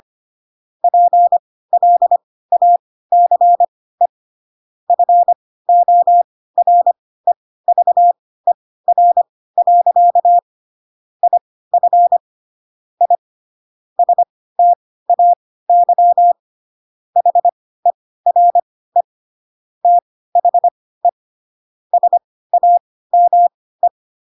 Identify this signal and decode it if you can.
Morse code, heard as a keyed tone, about 25 wpm, spaced as for 16 wpm, 700 Hz. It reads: PLACE FOREVER. IF I STAY HERE THE SAME
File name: Pinocchio / Pinocchio ch4 10 25 ik1hge